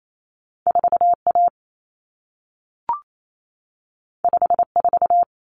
Morse code, heard as a keyed tone, about 28 wpm, 700 Hz. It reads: 4A 54
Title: Common Cut Numbers - 2-Digit 28wpm